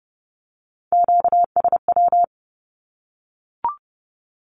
Morse code, heard as a keyed tone, about 30 wpm, 700 Hz. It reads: QSW
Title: Q-Codes 30wpm